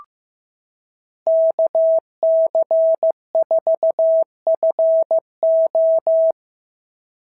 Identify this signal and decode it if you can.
Morse code, heard as a keyed tone, about 14 wpm, 650 Hz. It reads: KC4FO
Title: SOTA QSO Practice 15wpm